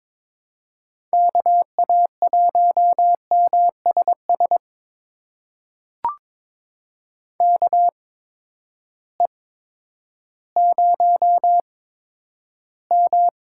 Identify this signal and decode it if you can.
Morse code, heard as a keyed tone, about 21 wpm, spaced as for 11 wpm, 700 Hz. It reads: KA1MSS KE0M
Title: USA Call Signs - 6 Characters - 8x Character Spacing - 22wpm